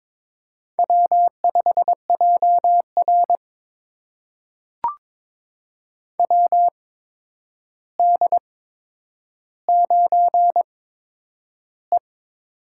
Morse code, heard as a keyed tone, about 22 wpm, spaced as for 9 wpm, 700 Hz. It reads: W5JR WD9E
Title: USA Call Signs - 4 Characters - 8x Character Spacing - 22wpm